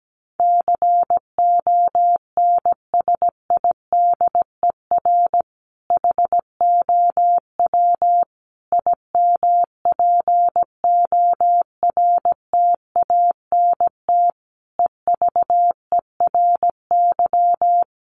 Morse code, heard as a keyed tone, about 17 wpm, 700 Hz. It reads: CONSIDER HOW IMPORTANT EVERY
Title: Sentences from Top 1000 Words 17wpm